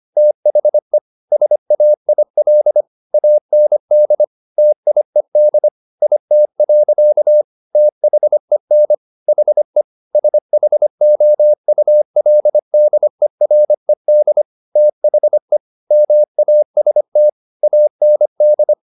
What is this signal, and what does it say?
Morse code, heard as a keyed tone, about 25 wpm, 600 Hz. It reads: THE SAIL AND TIED IT. THEN HE SHOULDERED THE MAST AND